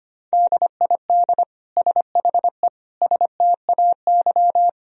Morse code, heard as a keyed tone, about 25 wpm, 700 Hz. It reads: DID SHE STAY
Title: Sentences from Top 500 Words and Abbreviations 25wpm